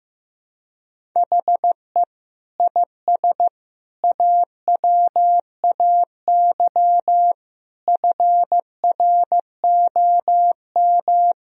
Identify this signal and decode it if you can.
Morse code, heard as a keyed tone, about 15 wpm, 700 Hz. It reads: HE IS AWAY FROM
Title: Sentences from Top 1000 Words - Review Part 1 - 15wpm